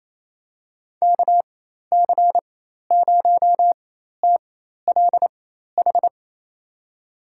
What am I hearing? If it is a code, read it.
Morse code, heard as a keyed tone, about 28 wpm, spaced as for 13 wpm, 700 Hz. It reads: KC0TLH
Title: USA Call Signs - 4x Character Spacing - 28wpm